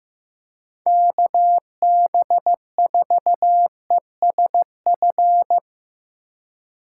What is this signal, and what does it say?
Morse code, heard as a keyed tone, about 17 wpm, 700 Hz. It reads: KB4ESF
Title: Speed Racing - USA 2x3 Call Signs 15wpm